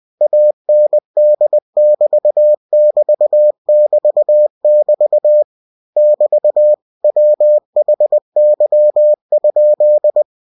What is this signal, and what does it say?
Morse code, heard as a keyed tone, about 20 wpm, 600 Hz. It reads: AND<BT><BT><BT><BT> <BT> WHY?
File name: winnie-20wpm